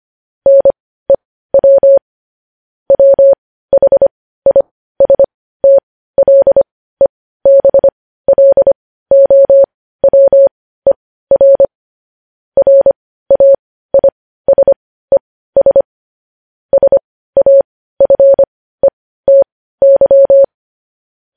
Morse code, heard as a keyed tone, about 25 wpm, spaced as for 15 wpm, 550 Hz. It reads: NEW WHISTLEBLOWER RAISES SAFETY